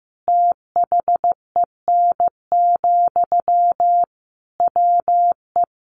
Morse code, heard as a keyed tone, about 15 wpm, 700 Hz. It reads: THEN, WE
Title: Sentences from Top 1500 Words 15wpm